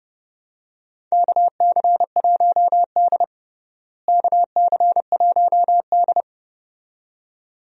Morse code, heard as a keyed tone, about 30 wpm, 700 Hz. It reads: KC1D KC1D
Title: USA Call Signs - 4 Characters - Repeated 2x - 30wpm